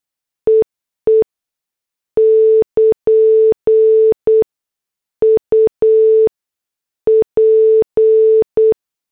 Morse code, heard as a keyed tone, about 8 wpm, spaced as for 7 wpm, 440 Hz. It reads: EE <KN> UP
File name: morse / headlines